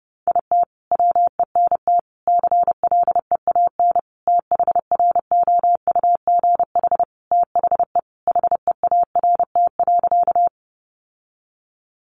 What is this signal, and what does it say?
Morse code, heard as a keyed tone, about 30 wpm, 700 Hz. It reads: IT WENT CLEAN THROUGH THE HEART.
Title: Sentences from Top 1200 Words 30wpm